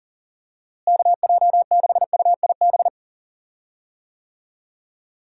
Morse code, heard as a keyed tone, about 40 wpm, 700 Hz. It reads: KJ6UIB
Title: USA Call Signs - 6 Characters - Repeated 3x - 40wpm